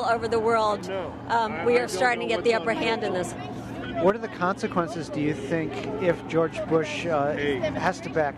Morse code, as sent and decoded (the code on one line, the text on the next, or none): none